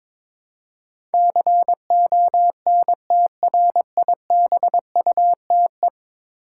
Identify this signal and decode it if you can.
Morse code, heard as a keyed tone, about 22 wpm, 700 Hz. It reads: CONTRIBUTE
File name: practice / 22wpm / 10 Letter Words - First 3 Letters Given - 22wpm